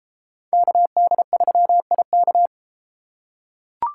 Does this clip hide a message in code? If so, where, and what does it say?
Yes: Morse code from 0.5 to 2.5 s, heard as a keyed tone, about 32 wpm, 700 Hz. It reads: KD3IK